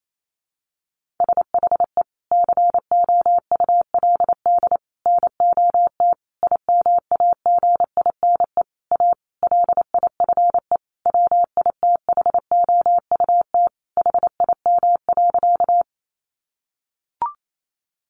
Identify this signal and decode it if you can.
Morse code, heard as a keyed tone, about 28 wpm, 700 Hz. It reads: SHE COULD NOT IMAGINE A LIFE WITHOUT HIM.
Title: Sentences from Top 1400 Words 28wpm